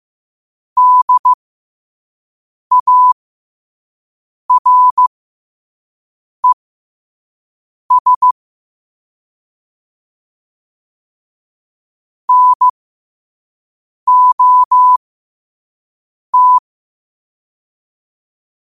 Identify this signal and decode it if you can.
Morse code, heard as a keyed tone, about 15 wpm, spaced as for 5 wpm, 1000 Hz. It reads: DARES NOT
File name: qotdfiles / 5wpm